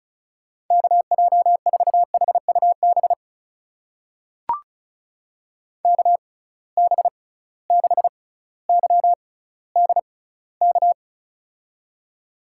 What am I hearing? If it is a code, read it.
Morse code, heard as a keyed tone, about 36 wpm, spaced as for 18 wpm, 700 Hz. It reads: KJ4HUB KB6YDK
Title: USA Call Signs - 6 Characters - 6x Character Spacing - 35wpm